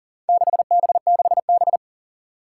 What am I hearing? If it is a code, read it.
Morse code, heard as a keyed tone, about 38 wpm, 700 Hz. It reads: 6B6B
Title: Four Challenging Letter-Number Combinations 40wpm